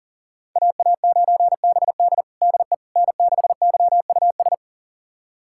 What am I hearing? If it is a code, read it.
Morse code, heard as a keyed tone, about 37 wpm, 700 Hz. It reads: AA9BD DE N6YUS
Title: USA Callsign DE Callsign 40wpm